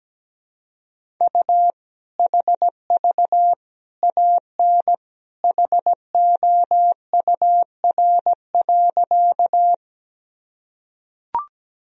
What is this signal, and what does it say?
Morse code, heard as a keyed tone, about 17 wpm, 700 Hz. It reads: U HV AN HOUR.
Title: Sentences from Top 400 Words and Abbreviations 17wpm